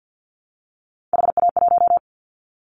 Morse code, heard as a keyed tone, about 50 wpm, 700 Hz. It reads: HA1
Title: Callsign Prefixes for Top 25 DXCC - Repeated 3x - 50wpm